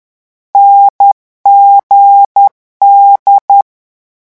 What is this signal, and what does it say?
Morse code, heard as a keyed tone, about 10 wpm, 800 Hz. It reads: NGD
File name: morse mp3 / lecon4-vitesse 10